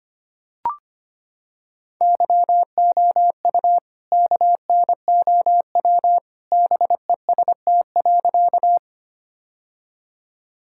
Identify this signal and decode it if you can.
Morse code, heard as a keyed tone, about 25 wpm, 700 Hz. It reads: YOU KNOW BEST.
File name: practice / 25wpm / Sentences with 2-Word Phrases - 25wpm